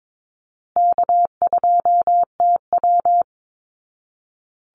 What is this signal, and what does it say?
Morse code, heard as a keyed tone, about 22 wpm, 700 Hz. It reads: K2TW
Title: Easy Call Signs Repeated 3x - 22wpm